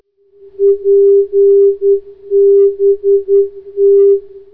Morse code, heard as a keyed tone, about 10 wpm, 400 Hz. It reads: PBT